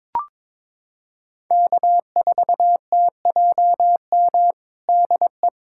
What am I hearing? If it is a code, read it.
Morse code, heard as a keyed tone, about 23 wpm, 700 Hz. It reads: K4TJM DE
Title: USA Callsign DE Callsign - 5 Characters - 22wpm